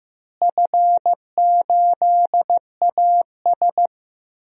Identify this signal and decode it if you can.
Morse code, heard as a keyed tone, about 15 wpm, 700 Hz. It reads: F8AS
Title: DXCC-100 Call Signs - 4 Characters - Repeated 3x - 15wpm